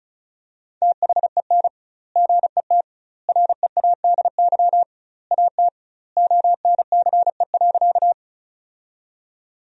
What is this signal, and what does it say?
Morse code, heard as a keyed tone, about 34 wpm, 700 Hz. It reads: THEN GET READY AT ONCE.
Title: 2x Word-Spacing - Sentences from Top 400 Words - 35wpm